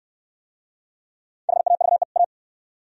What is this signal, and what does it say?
Morse code, heard as a keyed tone, about 69 wpm, 700 Hz. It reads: HAVE A